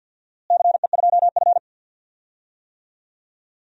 Morse code, heard as a keyed tone, about 48 wpm, 700 Hz. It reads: CE2F
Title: DXCC-100 Call Signs - 4 Characters - Repeated 3x - 50wpm